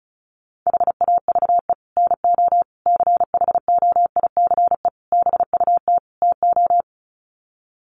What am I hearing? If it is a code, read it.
Morse code, heard as a keyed tone, about 34 wpm, 700 Hz. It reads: HAVE NO CHOICE BUT TO
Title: Sets of 5 Words Encore 35wpm